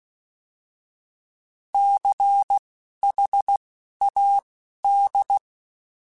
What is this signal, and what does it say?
Morse code, heard as a keyed tone, about 16 wpm, spaced as for 11 wpm, 800 Hz. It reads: CHAD
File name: Alphabetical / Alphabetical mp3s / Cs @10